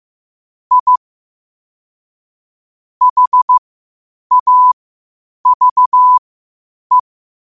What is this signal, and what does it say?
Morse code, heard as a keyed tone, about 15 wpm, spaced as for 8 wpm, 1000 Hz. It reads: I HAVE
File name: qotdfiles / 8wpm